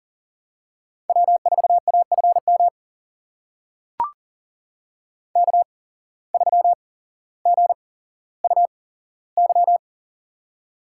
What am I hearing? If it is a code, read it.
Morse code, heard as a keyed tone, about 40 wpm, spaced as for 17 wpm, 700 Hz. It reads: W4AFM K3GUY
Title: USA Call Signs - 5 Characters - 8x Character Spacing - 40wpm